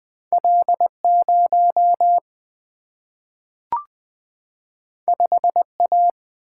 Morse code, heard as a keyed tone, about 20 wpm, 700 Hz. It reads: L 0 5 A
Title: Single-Letter Single-Number 20wpm